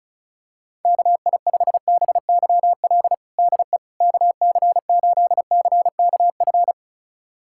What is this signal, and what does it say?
Morse code, heard as a keyed tone, about 36 wpm, 700 Hz. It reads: KI5BYL DE KC8CKF
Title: USA Callsign DE Callsign - 6 Characters - 35wpm